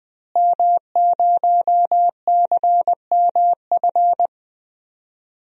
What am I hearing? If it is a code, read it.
Morse code, heard as a keyed tone, about 20 wpm, 700 Hz. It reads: M0CMF
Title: DXCC-100 Call Signs - 5 Characters - Repeated 2x - 20wpm